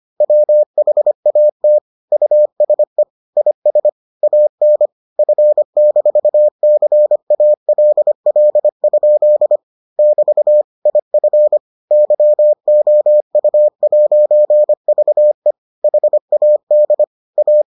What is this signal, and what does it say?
Morse code, heard as a keyed tone, about 25 wpm, 600 Hz. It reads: WHAT USE IS AN F-CALL? <BT> IF YOU'VE HAD A